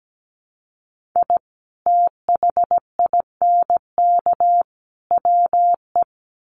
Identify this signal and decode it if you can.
Morse code, heard as a keyed tone, about 17 wpm, 700 Hz. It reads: I THINK WE
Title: Sets of 5 Words 17wpm